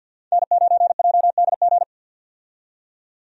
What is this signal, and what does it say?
Morse code, heard as a keyed tone, about 52 wpm, 700 Hz. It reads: N9JDG
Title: USA Call Signs - 5 Characters - 50wpm